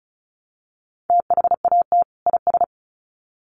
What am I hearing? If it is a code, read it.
Morse code, heard as a keyed tone, about 34 wpm, 700 Hz. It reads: THAT IS
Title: Sets of 2 Words 35wpm